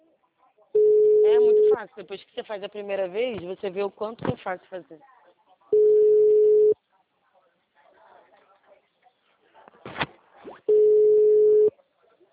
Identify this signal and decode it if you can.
Ringback tone (Europe, 425 Hz), with speech.
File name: out- -2005- - - .16982